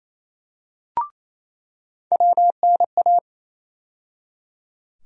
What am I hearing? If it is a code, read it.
Morse code, heard as a keyed tone, about 28 wpm, 700 Hz. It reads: WNA
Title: USA Call Sign Postfixes - 3 Letters 28wpm